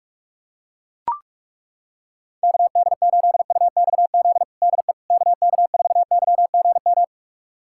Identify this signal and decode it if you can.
Morse code, heard as a keyed tone, about 41 wpm, 700 Hz. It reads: KD8UXZ DE KK4YGM
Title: USA Callsign DE Callsign - 6 Characters - 45wpm